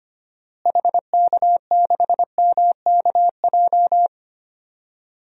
Morse code, heard as a keyed tone, about 25 wpm, 700 Hz. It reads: HK6MKJ
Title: DXCC-100 Call Signs - 6 Characters - Repeated 3x - 25wpm